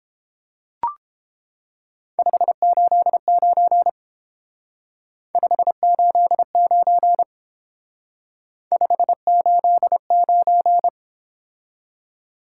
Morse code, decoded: 589 589 589

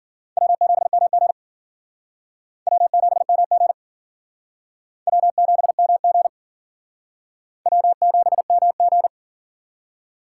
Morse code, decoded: W7MG W7MG W7MG W7MG

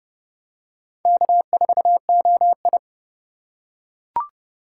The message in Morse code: K4OI

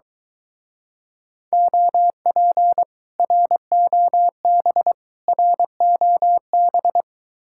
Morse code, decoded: OP ROB ROB

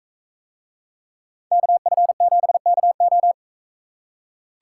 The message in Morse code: KF7KO

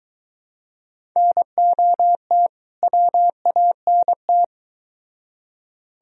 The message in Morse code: NOT WANT

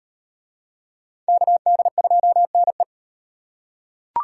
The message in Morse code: KD2NE

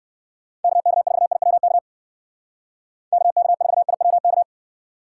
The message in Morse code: KK4IWK KK4IWK

